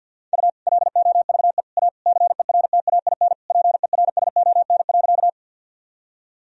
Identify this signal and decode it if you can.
Morse code, heard as a keyed tone, about 53 wpm, 700 Hz. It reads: U LOVE A CERTAIN PERSON.
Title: Sentences from Top 500 Words and Abbreviations 50wpm